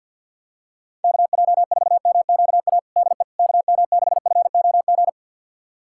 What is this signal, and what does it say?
Morse code, heard as a keyed tone, about 51 wpm, 700 Hz. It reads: KJ4MQA DE KM6FOG
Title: USA Callsign DE Callsign - 6 Characters - 50wpm